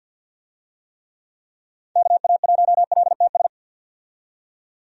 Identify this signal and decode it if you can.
Morse code, heard as a keyed tone, about 49 wpm, 700 Hz. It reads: KA1LTS
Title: USA Call Signs - 6 Characters - 50wpm